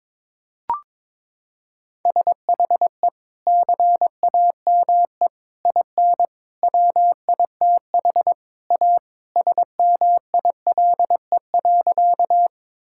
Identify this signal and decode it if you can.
Morse code, heard as a keyed tone, about 21 wpm, 700 Hz. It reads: SHE CAME IN WITH A SMILE.
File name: practice / 22wpm / Sentences with 2-Word Phrases - 22wpm